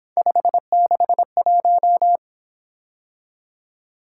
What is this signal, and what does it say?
Morse code, heard as a keyed tone, about 26 wpm, 700 Hz. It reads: 561
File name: Speed-Racing - Three Letter-Number Combinations - 22wpm